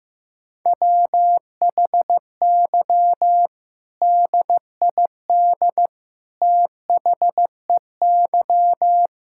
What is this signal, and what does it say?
Morse code, heard as a keyed tone, about 15 wpm, 700 Hz. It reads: WHY DID THEY
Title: Sentences from Top 1000 Words - Review Part 6 - 15wpm